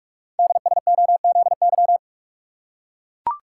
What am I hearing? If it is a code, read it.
Morse code, heard as a keyed tone, about 44 wpm, 700 Hz. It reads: DSOZY
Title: Code Groups - 5 Letters - 45wpm